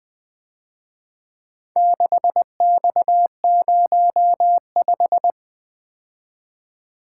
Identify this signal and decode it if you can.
Morse code, heard as a keyed tone, about 24 wpm, 700 Hz. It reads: 6X05
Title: Speed-Racing - Four Letter-Number Combinations - 20wpm